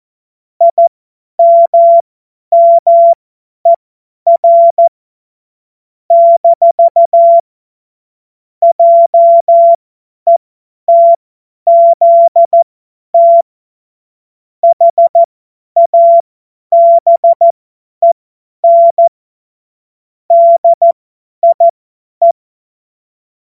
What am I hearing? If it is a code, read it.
Morse code, heard as a keyed tone, about 14 wpm, spaced as for 10 wpm, 680 Hz. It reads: IMMER - JETZT HABEN DIE
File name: Morsekurs / Witze / Witz 07. Februar 2023 50 F70